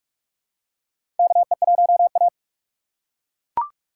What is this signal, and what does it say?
Morse code, heard as a keyed tone, about 41 wpm, 700 Hz. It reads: KE1A
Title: USA Call Signs - 4 Characters - 4x Character Spacing - 45wpm